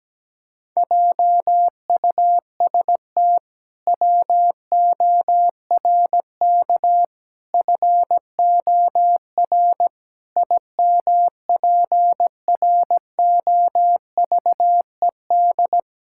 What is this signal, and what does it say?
Morse code, heard as a keyed tone, about 17 wpm, 700 Hz. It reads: JUST WORK FOR IMPROVED
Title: Pangram Challenge v2 - 17wpm